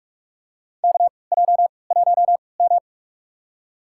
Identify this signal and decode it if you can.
Morse code, heard as a keyed tone, about 42 wpm, spaced as for 25 wpm, 700 Hz. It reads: KJ1M
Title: USA Call Signs - 3x Character Spacing - 45wpm